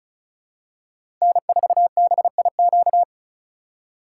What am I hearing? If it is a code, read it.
Morse code, heard as a keyed tone, about 36 wpm, 700 Hz. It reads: N4BIQ
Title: USA Call Signs - 5 Characters - 8x Character Spacing - 35wpm